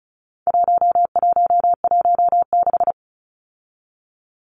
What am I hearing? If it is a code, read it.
Morse code, heard as a keyed tone, about 34 wpm, 700 Hz. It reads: 1116
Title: Numbers - 4-Digit 35wpm